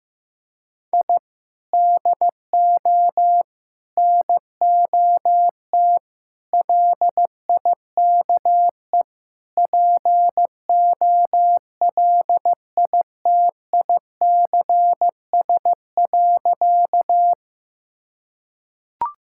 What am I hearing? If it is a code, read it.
Morse code, heard as a keyed tone, about 15 wpm, 700 Hz. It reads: I DO NOT LIKE POLITICS.
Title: Sentences from Top 1000 Words - Review Part 6 - 15wpm